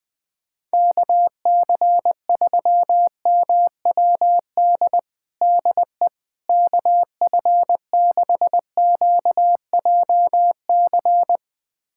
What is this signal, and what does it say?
Morse code, heard as a keyed tone, about 20 wpm, 700 Hz. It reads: KC3MWD DE KF6QJC
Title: USA Callsign DE Callsign 20wpm